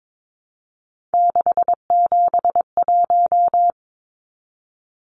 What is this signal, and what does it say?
Morse code, heard as a keyed tone, about 22 wpm, 700 Hz. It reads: 671
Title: Numbers - 3-Digit 22wpm